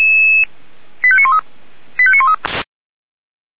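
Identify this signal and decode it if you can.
Five-tone selective call (ZVEI), heard as quick runs of five tones, tones of about 70 ms. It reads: 85913 85913
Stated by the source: ZVEI five-tone selective call received on SDR.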